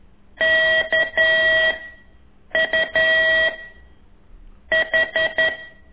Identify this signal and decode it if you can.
Morse code, heard as a keyed tone, about 11 wpm, 1950 Hz. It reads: K U H
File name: ik1weg2 / CORSI CW